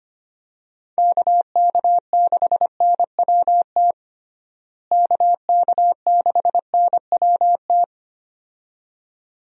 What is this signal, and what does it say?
Morse code, heard as a keyed tone, about 25 wpm, 700 Hz. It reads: KK6NWT KK6NWT
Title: USA Call Signs - 6 Characters - Repeated 2x - 25wpm